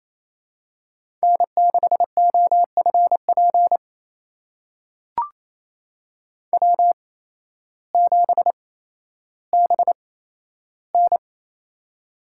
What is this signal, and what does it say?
Morse code, heard as a keyed tone, about 27 wpm, spaced as for 13 wpm, 700 Hz. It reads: N6OFP W7BN
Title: USA Call Signs - 8x Character Spacing - 28wpm